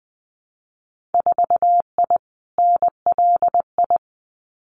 Morse code, heard as a keyed tone, about 20 wpm, 700 Hz. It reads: 4I NLI
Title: Winter Field Day Exchanges Encore 20wpm